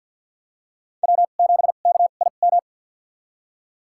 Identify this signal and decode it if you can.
Morse code, heard as a keyed tone, about 46 wpm, spaced as for 34 wpm, 700 Hz. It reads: W7KIM
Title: USA Call Signs - 5 Characters - 2x Character Spacing - 50wpm